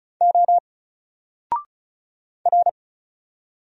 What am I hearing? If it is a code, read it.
Morse code, heard as a keyed tone, about 34 wpm, 700 Hz. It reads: O R
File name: Single-Letter-Number-35wpm